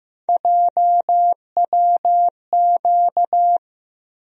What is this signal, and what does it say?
Morse code, heard as a keyed tone, about 15 wpm, 700 Hz. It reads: JWQ
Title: USA Call Sign Postfixes - 3 Letters 15wpm